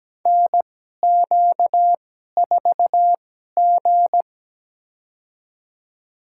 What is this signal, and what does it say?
Morse code, heard as a keyed tone, about 17 wpm, spaced as for 12 wpm, 700 Hz. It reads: NQ4G
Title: USA Call Signs - 2x Character Spacing - 17wpm